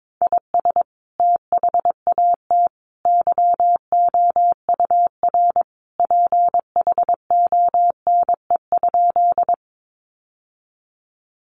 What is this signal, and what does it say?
Morse code, heard as a keyed tone, about 22 wpm, 700 Hz. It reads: IS THAT YOUR PHONE?
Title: Sentences from Top 1100 Words 22wpm